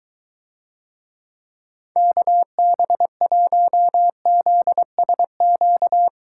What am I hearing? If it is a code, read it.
Morse code, heard as a keyed tone, about 22 wpm, 700 Hz. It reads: KB1ZSQ